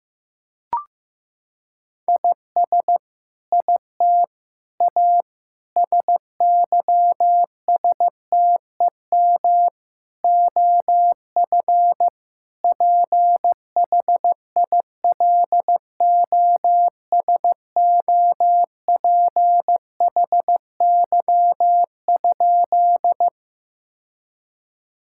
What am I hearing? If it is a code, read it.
Morse code, heard as a keyed tone, about 15 wpm, 700 Hz. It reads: IS IT A SYSTEM OF PHILOSOPHY?
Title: Sentences from Top 1000 Words - Review Part 6 - 15wpm